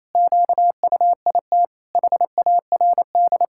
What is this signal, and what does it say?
Morse code, heard as a keyed tone, about 28 wpm, 700 Hz. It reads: QUIT HARD
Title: Pangram Challenge v2 - 28wpm